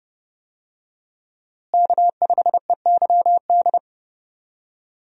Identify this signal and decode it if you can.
Morse code, heard as a keyed tone, about 30 wpm, 700 Hz. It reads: K5EYD
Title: USA Call Signs - 5 Characters - Repeated 2x - 30wpm